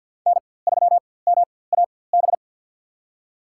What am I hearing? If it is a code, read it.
Morse code, heard as a keyed tone, about 51 wpm, spaced as for 23 wpm, 700 Hz. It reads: N3MAB
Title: USA Call Signs - 4x Character Spacing - 50wpm